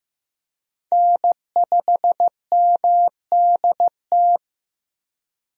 Morse code, heard as a keyed tone, about 15 wpm, 700 Hz. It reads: N5MDT
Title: USA Call Signs - 5 Characters - 15wpm